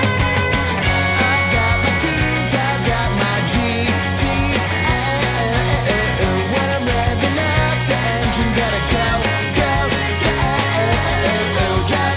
Dial-up modem handshake.